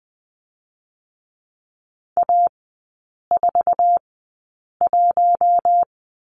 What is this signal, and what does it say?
Morse code, heard as a keyed tone, about 20 wpm, spaced as for 8 wpm, 690 Hz. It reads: A41
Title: 060 Lesson 11e - Letter h - Three Characters 20wpm